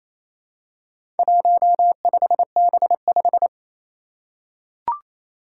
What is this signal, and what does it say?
Morse code, heard as a keyed tone, about 27 wpm, 700 Hz. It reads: 15B5
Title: Four Challenging Letter-Number Combinations 28wpm